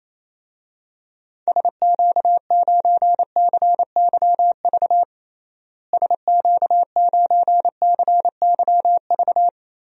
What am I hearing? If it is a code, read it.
Morse code, heard as a keyed tone, about 28 wpm, 700 Hz. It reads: SQ9CYV SQ9CYV
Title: DXCC-100 Call Signs - 6 Characters - Repeated 3x - 28wpm